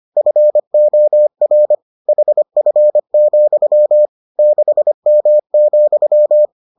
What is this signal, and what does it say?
Morse code, heard as a keyed tone, about 25 wpm, 600 Hz. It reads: FOR HF, 6M,